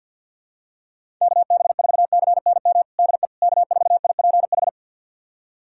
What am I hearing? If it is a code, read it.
Morse code, heard as a keyed tone, about 52 wpm, 700 Hz. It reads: KB4CNM DE K4IPH
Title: USA Callsign DE Callsign 50wpm